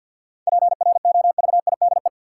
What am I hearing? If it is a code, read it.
Morse code, heard as a keyed tone, about 51 wpm, 700 Hz. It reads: PROVIDE
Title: Sentences from Top 1000 Words - Review Part 2 - New Word Drill - 50wpm